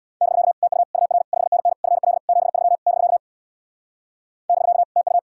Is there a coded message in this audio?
Yes: Morse code from 0.2 to 5.2 s, heard as a keyed tone, about 64 wpm, 700 Hz. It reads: <BK> TU UR 5NN FL 73 <BK> <BK> TU